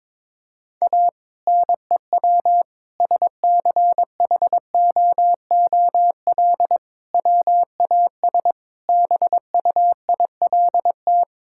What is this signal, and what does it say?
Morse code, heard as a keyed tone, about 22 wpm, 700 Hz. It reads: A NEW SCHOOL WAS BUILT
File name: practice / 22wpm / Sentences with 2-Word Phrases - 22wpm